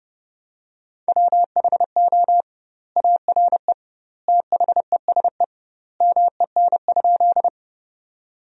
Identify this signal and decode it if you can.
Morse code, heard as a keyed tone, about 30 wpm, 700 Hz. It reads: WHO ARE THESE MEN?
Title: 2x Word-Spacing - Sentences from Top 200 Words - 30wpm